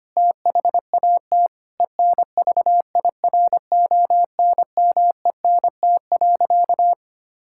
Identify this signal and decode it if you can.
Morse code, heard as a keyed tone, about 25 wpm, 700 Hz. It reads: THAT ENVIRONMENT.
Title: Sentences from Top 1000 Words - Review Part 5 - 25wpm